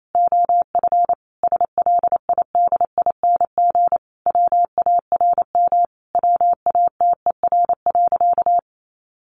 Morse code, heard as a keyed tone, about 28 wpm, 700 Hz. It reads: OF SLIDING WARM WATER.
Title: A Sound Of Thunder 28wpm